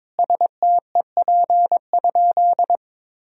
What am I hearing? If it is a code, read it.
Morse code, heard as a keyed tone, about 22 wpm, 700 Hz. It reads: STEP?
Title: Sentences from Top 1000 Words - Review Part 2 - 22wpm